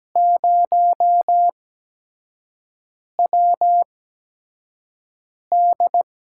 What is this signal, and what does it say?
Morse code, heard as a keyed tone, about 17 wpm, spaced as for 5 wpm, 700 Hz. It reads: 0WD
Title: USA Call Signs - 4 Characters - 8x Character Spacing - 17wpm